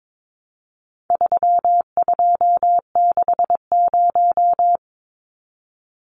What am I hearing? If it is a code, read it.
Morse code, heard as a keyed tone, about 22 wpm, 700 Hz. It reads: 3260
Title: Numbers - 4-Digit 22wpm